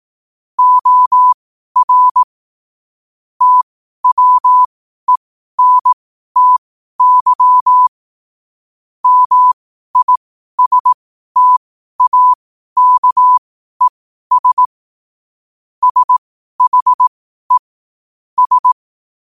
Morse code, heard as a keyed tone, about 18 wpm, spaced as for 12 wpm, 1000 Hz. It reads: OR TWENTY MISTAKES SHE S
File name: qotdfiles / 12wpm